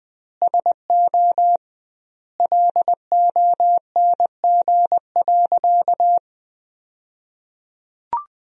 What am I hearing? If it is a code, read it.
Morse code, heard as a keyed tone, about 21 wpm, 700 Hz. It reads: SO LONG.
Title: 2x Word-Spacing - Sentences from Top 100 Words - 20wpm